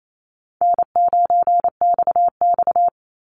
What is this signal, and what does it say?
Morse code, heard as a keyed tone, about 28 wpm, 700 Hz. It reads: N9XX
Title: Easy Call Signs Repeated 2x - 28wpm